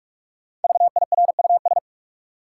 Morse code, heard as a keyed tone, about 41 wpm, 700 Hz. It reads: VIRUS
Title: Five Letter Words - 45wpm